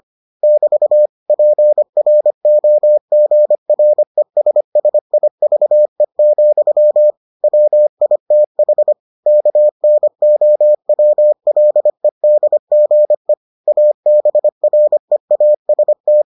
Morse code, heard as a keyed tone, about 25 wpm, 600 Hz. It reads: <BT> PROGRESSIVE, WITH KNOWLEDGE ABREAST